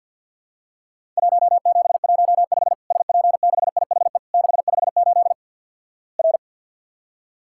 Morse code, heard as a keyed tone, about 50 wpm, 700 Hz. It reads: 1715 SP6IHE 658 R
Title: Worked All Europe DX Contest - QTC Exchange with Repeats - 50wpm